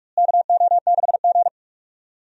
Morse code, heard as a keyed tone, about 43 wpm, 700 Hz. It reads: KO6G
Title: USA Call Signs - 4 Characters - Repeated 3x - 45wpm